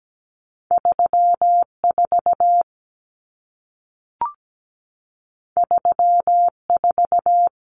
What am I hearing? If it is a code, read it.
Morse code, heard as a keyed tone, about 17 wpm, 700 Hz. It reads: 34 34
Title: Common Cut Numbers - 2-Digit 17wpm